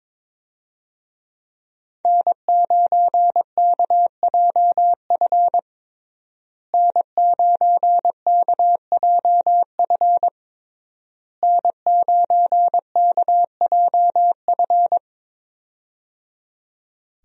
Morse code, heard as a keyed tone, about 22 wpm, 700 Hz. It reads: N9KJF N9KJF N9KJF
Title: USA Call Signs - 5 Characters - Repeated 3x - 22wpm